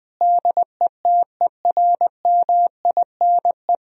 Morse code, heard as a keyed tone, about 20 wpm, 700 Hz. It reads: DETERMINE
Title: Sentences from Top 1000 Words - Review Part 4 - 20wpm